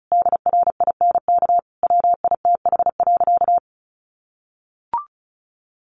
Morse code, heard as a keyed tone, about 34 wpm, 700 Hz. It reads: DRINK WITH.
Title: Sentences from Top 1500 Words 35wpm